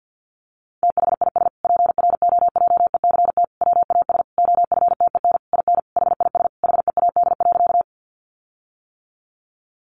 Morse code, heard as a keyed tone, about 50 wpm, 700 Hz. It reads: THIS PROJECT WAS OFTEN IN HIS HEAD.
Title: Sentences from Top 1400 Words 50wpm